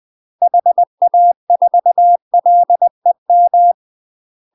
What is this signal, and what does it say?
Morse code, heard as a keyed tone, about 20 wpm, 700 Hz. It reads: HA4LEM
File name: UK MASTER QSOs TRAINING / 20WPM / UK MASTER QSOs 0046